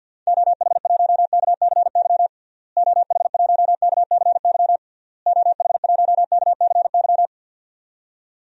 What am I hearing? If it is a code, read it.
Morse code, heard as a keyed tone, about 46 wpm, 700 Hz. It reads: OH1KCY OH1KCY OH1KCY
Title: DXCC-100 Call Signs - 6 Characters - Repeated 3x - 50wpm